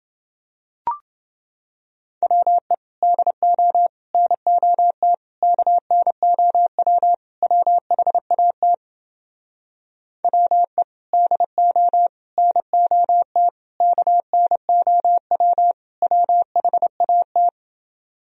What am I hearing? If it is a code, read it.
Morse code, heard as a keyed tone, about 30 wpm, 700 Hz. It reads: WE DO NOT KNOW WHAT WE DO NOT KNOW WHAT